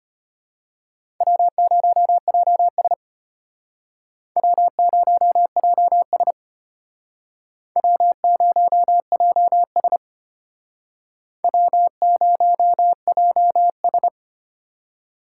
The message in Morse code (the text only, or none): W0JS W0JS W0JS W0JS